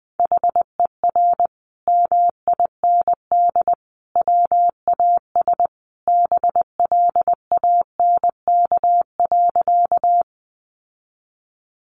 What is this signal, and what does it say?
Morse code, heard as a keyed tone, about 20 wpm, 700 Hz. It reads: HER MIND WAS BLANK.
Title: Sentences from Top 1500 Words 20wpm